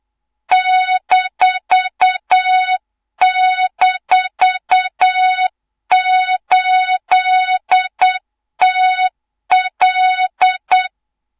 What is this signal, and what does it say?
Morse code, heard as a keyed tone, about 8 wpm, 3700 Hz, with noise. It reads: --8TL
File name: Morze-HA2MN / hang / 24 GYAKORLAT